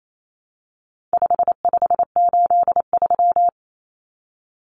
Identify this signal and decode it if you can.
Morse code, heard as a keyed tone, about 28 wpm, 700 Hz. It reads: 5583
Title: Common Cut Numbers - 4-Digit 28wpm